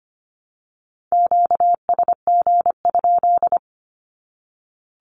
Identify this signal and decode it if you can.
Morse code, heard as a keyed tone, about 25 wpm, 700 Hz. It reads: QSG?
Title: Q-Codes 25wpm